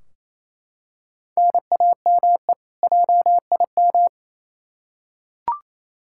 Morse code, heard as a keyed tone, about 28 wpm, 700 Hz. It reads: NAME JIM
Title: K1USN SST QSO Practice 28wpm